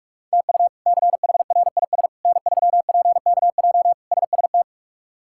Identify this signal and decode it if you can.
Morse code, heard as a keyed tone, about 46 wpm, 700 Hz. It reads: TU CHRIS N3PKJ SST
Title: K1USN SST QSO Practice - 45wpm